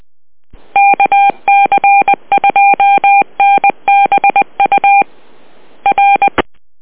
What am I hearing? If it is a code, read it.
Morse code, heard as a keyed tone, about 20 wpm, 800 Hz. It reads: KC2NBU R
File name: net audio